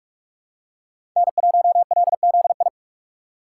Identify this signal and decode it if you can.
Morse code, heard as a keyed tone, about 47 wpm, 700 Hz. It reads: N1LZI
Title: USA Call Signs - 5 Characters - 4x Character Spacing - 45wpm